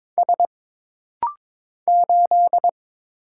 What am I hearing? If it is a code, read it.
Morse code, heard as a keyed tone, about 22 wpm, 700 Hz. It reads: S 8
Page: This rapid-fire timing is geared to push your ICR (Instant Character Recognition) skills.